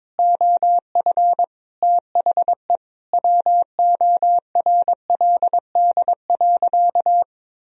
Morse code, heard as a keyed tone, about 22 wpm, 700 Hz. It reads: OF THE WORLD.
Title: Sentences from Top 1000 Words - Review Part 1 - 22wpm